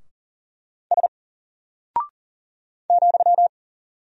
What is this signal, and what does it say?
Morse code, heard as a keyed tone, about 39 wpm, 700 Hz. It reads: S ,
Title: Single Character (Letter:Number:Punctuation:Prosign) Alt 40wpm